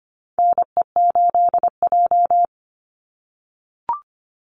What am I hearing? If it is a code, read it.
Morse code, heard as a keyed tone, about 25 wpm, 700 Hz. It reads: NE8J
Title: Call Signs - 25wpm